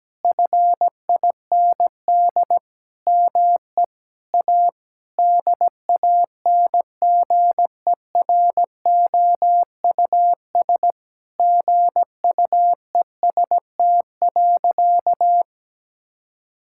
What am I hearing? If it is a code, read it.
Morse code, heard as a keyed tone, about 17 wpm, 700 Hz. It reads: FIND ME A DANGEROUS GUEST.
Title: Sentences from Top 1600 Words 17wpm